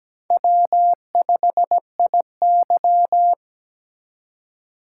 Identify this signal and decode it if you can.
Morse code, heard as a keyed tone, about 17 wpm, 700 Hz. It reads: W5IY
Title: USA Call Signs - 4 Characters - 2x Character Spacing - 17wpm